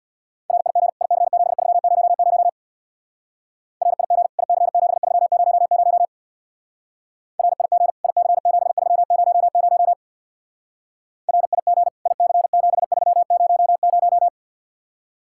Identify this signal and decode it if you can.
Morse code, heard as a keyed tone, about 58 wpm, 700 Hz. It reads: RIG IC7300 RIG IC7300 RIG IC7300 RIG IC7300
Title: Speed-Racing Common QSO Phrases - 50wpm